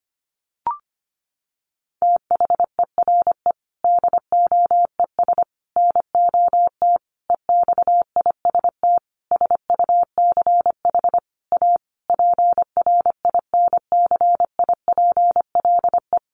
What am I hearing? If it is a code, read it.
Morse code, heard as a keyed tone, about 25 wpm, 700 Hz. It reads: THERE DOES NOT EXIST SUCH A PRINCIPLE
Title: Sentences from Top 1400 Words 25wpm